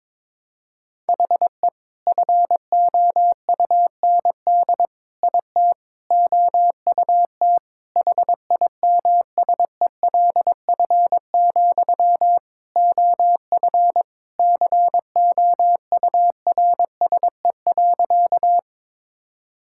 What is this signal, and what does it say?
Morse code, heard as a keyed tone, about 21 wpm, 700 Hz. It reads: HE FOUND IT OUT HIMSELF, OF COURSE.
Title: Sentences from Top 1600 Words 22wpm